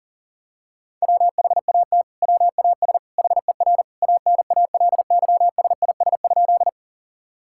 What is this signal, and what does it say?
Morse code, heard as a keyed tone, about 40 wpm, 700 Hz. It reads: WHAT WAS HER ANALYSIS?
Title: Sentences from Top 1000 Words - Review Part 5 - 40wpm